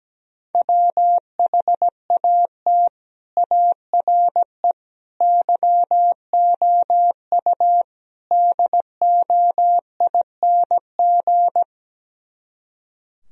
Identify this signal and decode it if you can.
Morse code, heard as a keyed tone, about 17 wpm, 700 Hz. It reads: WHAT ARE YOU DOING
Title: Speed-Racing - Sets of 4 Words 17wpm